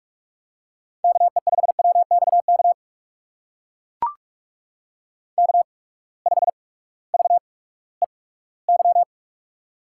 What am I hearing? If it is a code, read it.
Morse code, heard as a keyed tone, about 44 wpm, spaced as for 20 wpm, 700 Hz. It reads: KE5WXK K5VEY
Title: USA Call Signs - 8x Character Spacing - 45wpm